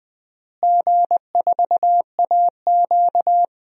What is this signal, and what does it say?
Morse code, heard as a keyed tone, about 20 wpm, 700 Hz. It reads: G4AQ
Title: DXCC-100 Call Signs - 4 Characters - Repeated 3x - 20wpm